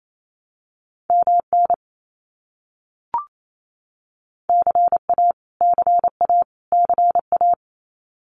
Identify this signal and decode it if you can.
Morse code, heard as a keyed tone, about 28 wpm, 700 Hz. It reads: MN CA CA CA
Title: US States Repeated 3x 28wpm